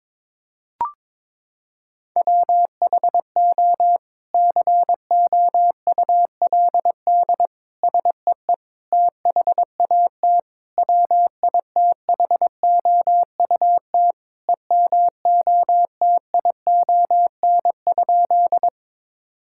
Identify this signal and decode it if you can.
Morse code, heard as a keyed tone, about 22 wpm, 700 Hz. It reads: WHO COULD SEE THAT WITHOUT EMOTION?
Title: Sentences from Top 1600 Words 22wpm